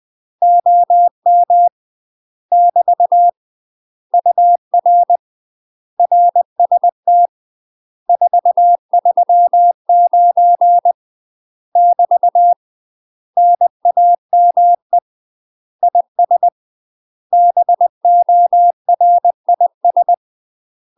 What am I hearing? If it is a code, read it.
Morse code, heard as a keyed tone, about 20 wpm, 700 Hz. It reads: OM <BT> UR RST 439 <BT> NAME IS BORIS
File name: UK MASTER QSOs TRAINING / 20WPM / UK MASTER QSOs 0045